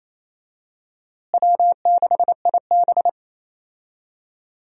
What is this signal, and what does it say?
Morse code, heard as a keyed tone, about 28 wpm, 700 Hz. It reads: W6IB
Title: USA Call Signs - 4 Characters - Repeated 3x - 28wpm